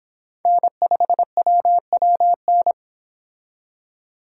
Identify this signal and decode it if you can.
Morse code, heard as a keyed tone, about 26 wpm, 700 Hz. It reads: N5WWN